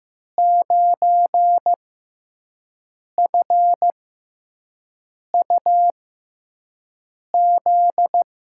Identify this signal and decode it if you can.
Morse code, heard as a keyed tone, about 15 wpm, spaced as for 5 wpm, 700 Hz. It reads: 9FUZ
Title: USA Call Signs - 6x Character Spacing - 15wpm